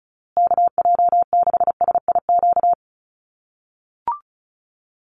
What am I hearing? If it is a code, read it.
Morse code, heard as a keyed tone, about 34 wpm, 700 Hz. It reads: KJ6SIQ
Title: USA Call Signs Repeated 2x 35wpm